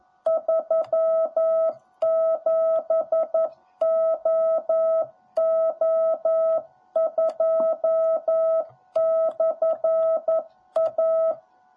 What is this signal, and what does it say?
Morse code, heard as a keyed tone, about 11 wpm, 650 Hz. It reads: 37OO2/A
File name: jh8cbh / 2021cw / cw10b